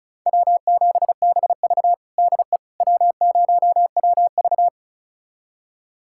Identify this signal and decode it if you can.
Morse code, heard as a keyed tone, about 36 wpm, 700 Hz. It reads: W7BV DE W0WV
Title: USA Callsign DE Callsign - 4 Characters - 35wpm